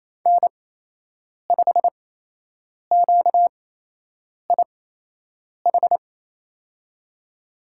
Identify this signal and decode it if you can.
Morse code, heard as a keyed tone, about 27 wpm, spaced as for 8 wpm, 700 Hz. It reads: N5QIH
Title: USA Call Signs - 5 Characters - 8x Character Spacing - 28wpm